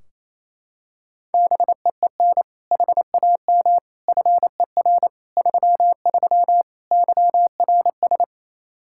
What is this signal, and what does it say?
Morse code, heard as a keyed tone, about 27 wpm, 700 Hz. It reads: BEEN HAM FER 33 YRS
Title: Practice QSO - Exchange 3 - More Info - 28wpm